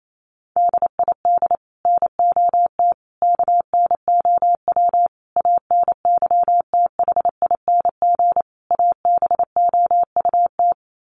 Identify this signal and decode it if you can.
Morse code, heard as a keyed tone, about 28 wpm, 700 Hz. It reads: DID NOT KNOW ANYTHING ABOUT
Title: Sets of 5 Words Encore2 28wpm